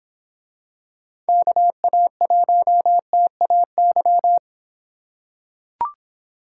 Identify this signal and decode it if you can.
Morse code, heard as a keyed tone, about 24 wpm, 700 Hz. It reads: KA1TAY